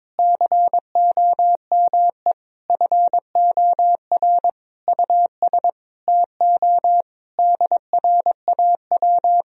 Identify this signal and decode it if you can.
Morse code, heard as a keyed tone, about 22 wpm, 700 Hz. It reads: COME FOR US TO DRAW
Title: Sentences from Top 1000 Words - Review Part 1 - 22wpm